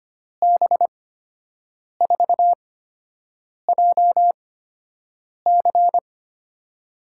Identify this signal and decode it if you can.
Morse code, heard as a keyed tone, about 25 wpm, spaced as for 7 wpm, 700 Hz. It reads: B4JC
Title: USA Call Signs - 5 Characters - 8x Character Spacing - 25wpm